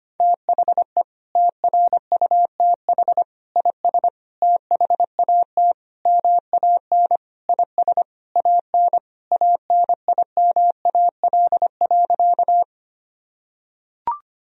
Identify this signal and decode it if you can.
Morse code, heard as a keyed tone, about 25 wpm, 700 Hz. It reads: THE TRUTH IS THAT MAN IS AN ANIMAL.
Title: Sentences from Top 1000 Words - Review Part 1 - 25wpm